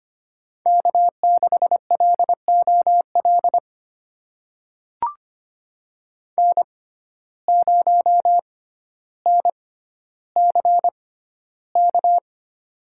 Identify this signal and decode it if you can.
Morse code, heard as a keyed tone, about 25 wpm, spaced as for 13 wpm, 700 Hz. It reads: K6LOL N0NCK
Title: USA Call Signs - 5 Characters - 6x Character Spacing - 25wpm